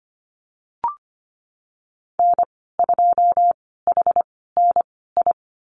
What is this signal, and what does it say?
Morse code, heard as a keyed tone, about 25 wpm, spaced as for 16 wpm, 700 Hz. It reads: N2HNI
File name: Lesson 15g - Number 5 - Call Signs 25wpm